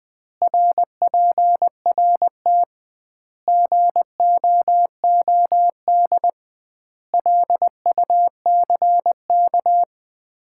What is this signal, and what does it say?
Morse code, heard as a keyed tone, about 20 wpm, 700 Hz. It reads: RPRT GOOD LUCK
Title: Common QSO Phrases - 2x Word-Spacing - 20wpm